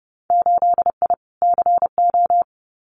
Morse code, heard as a keyed tone, about 30 wpm, 700 Hz. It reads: 8I CO
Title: Winter Field Day Exchanges Encore 30wpm